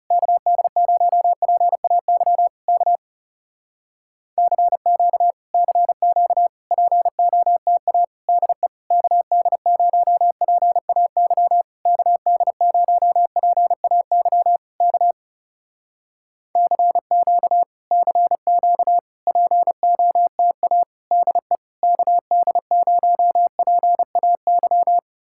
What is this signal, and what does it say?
Morse code, heard as a keyed tone, about 35 wpm, 700 Hz. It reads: KD0PAY K CQ CQ POTA DE KD0PAY KD0PAY K CQ CQ POTA DE KD0PAY
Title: Speed-Racing Common QSO Phrases - 30wpm